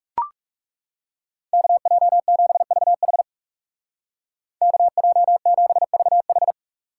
Morse code, heard as a keyed tone, about 35 wpm, 700 Hz. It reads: KJ7VH KJ7VH
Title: Speed Racing - USA 2x2 Call Signs 30wpm